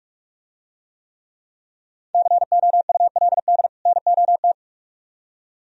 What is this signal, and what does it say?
Morse code, heard as a keyed tone, about 44 wpm, 700 Hz. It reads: COULD NOT